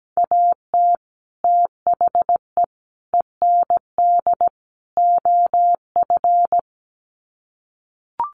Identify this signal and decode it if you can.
Morse code, heard as a keyed tone, about 17 wpm, 700 Hz. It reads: AT THE END OF
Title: Sets of 4 Words 17wpm